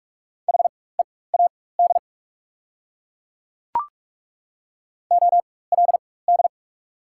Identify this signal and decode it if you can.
Morse code, heard as a keyed tone, about 43 wpm, spaced as for 21 wpm, 690 Hz. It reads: HEAD OLD
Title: 4x Character Spacing - Top 300 Words - 45wpm